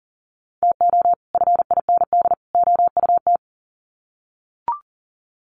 Morse code, heard as a keyed tone, about 40 wpm, 700 Hz. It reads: TO FIND OUT
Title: Sets of 3 Words 40wpm